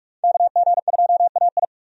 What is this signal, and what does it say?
Morse code, heard as a keyed tone, about 46 wpm, 700 Hz. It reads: KG2AI
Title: USA Call Signs - 5 Characters - 45wpm